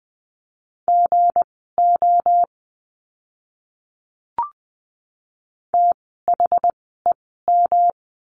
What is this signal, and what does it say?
Morse code, heard as a keyed tone, about 20 wpm, spaced as for 14 wpm, 700 Hz. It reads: GO THEM
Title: 244 Lesson 39f - Letter z - Words 20wpm